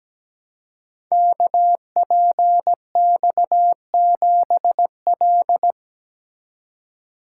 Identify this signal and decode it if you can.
Morse code, heard as a keyed tone, about 17 wpm, 700 Hz. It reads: KPX7L
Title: Code Groups - 5 Letter:Numbers - 17wpm